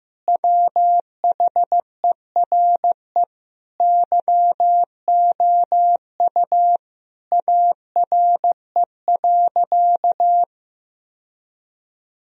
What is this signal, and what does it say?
Morse code, heard as a keyed tone, about 15 wpm, 700 Hz. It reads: WHERE YOU ARE.
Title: Sentences from Top 1000 Words - Review Part 1 - 15wpm